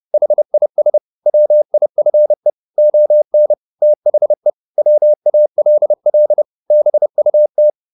Morse code, heard as a keyed tone, about 30 wpm, 600 Hz. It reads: HIS WIFE ON THE WALL BUT
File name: the old man and the sea / the old man and the sea 1 30wpm